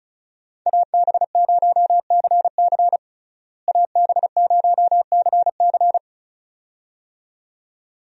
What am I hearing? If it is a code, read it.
Morse code, heard as a keyed tone, about 34 wpm, 700 Hz. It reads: AB0CC AB0CC
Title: USA Call Signs - 5 Characters - Repeated 2x - 35wpm